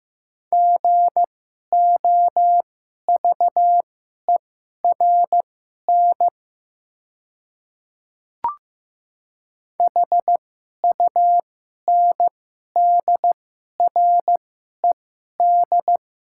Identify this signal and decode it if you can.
Morse code, heard as a keyed tone, about 15 wpm, spaced as for 11 wpm, 690 Hz. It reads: GOVERN HUNDRED
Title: 2x Character Spacing - Top 500 Words - 15wpm